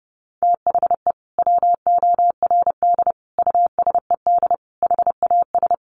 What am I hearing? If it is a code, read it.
Morse code, heard as a keyed tone, about 30 wpm, 700 Hz. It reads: THE WORD USED HAS
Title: Sentences from Top 1100 Words 30wpm